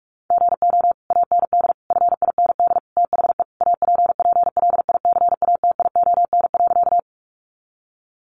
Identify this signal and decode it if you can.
Morse code, heard as a keyed tone, about 44 wpm, 700 Hz. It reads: GO AND FIND THE APPLICATION.
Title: Sentences from Top 1200 Words 45wpm